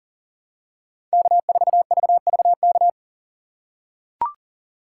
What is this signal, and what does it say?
Morse code, heard as a keyed tone, about 40 wpm, 700 Hz. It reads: K4VVK
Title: USA Call Signs - 5 Characters - 4x Character Spacing - 40wpm